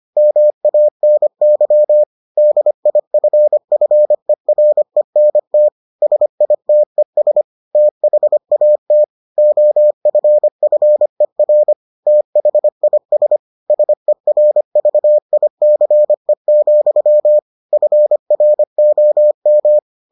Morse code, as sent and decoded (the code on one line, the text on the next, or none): -- .- -. -.-- / -.. .. ..-. ..-. .-. -. - / ... .. - . ... / - .... .- - / --- ..-. ..-. .-. / - .... .. ... / ... .-. ...- .. -.-. --..-- / ..-. .-. --- --
MANY DIFFERENT SITES THAT OFFER THIS SERVICE, FROM